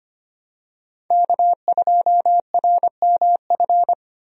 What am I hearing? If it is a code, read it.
Morse code, heard as a keyed tone, about 25 wpm, 700 Hz. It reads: K2RMF
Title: USA Call Signs - 5 Characters - Repeated 2x - 25wpm